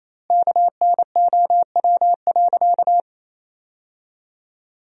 Morse code, heard as a keyed tone, about 28 wpm, 700 Hz. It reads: KNOW.
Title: 2x Word-Spacing - Sentences from Top 300 Words - 28wpm